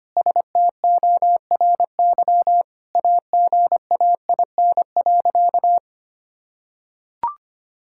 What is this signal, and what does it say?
Morse code, heard as a keyed tone, about 25 wpm, 700 Hz. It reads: STORY AGAIN.
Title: Sentences from Top 1000 Words - Review Part 5 - 25wpm